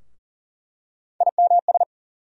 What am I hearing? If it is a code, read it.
Morse code, heard as a keyed tone, about 37 wpm, 700 Hz. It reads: IMS
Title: USA Call Sign Postfixes - 3 Letters 40wpm